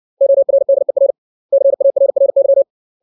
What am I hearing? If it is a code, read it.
Morse code, heard as a keyed tone, about 60 wpm, 550 Hz. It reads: ORDER CARRY
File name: 632 words / 632 Words 60WPM